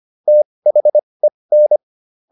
Morse code, heard as a keyed tone, about 25 wpm, spaced as for 20 wpm, 600 Hz. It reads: THEN